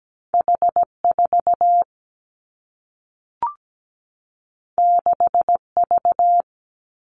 Morse code, heard as a keyed tone, about 17 wpm, 700 Hz. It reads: H4 6V
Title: Two Challenging Letter-Number Combinations 17wpm